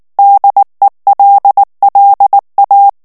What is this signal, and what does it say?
Morse code, heard as a keyed tone, about 19 wpm, 800 Hz. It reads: DELLA
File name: morse mp3 / ITALIEN texte clair 2 vitesse 18 mots minute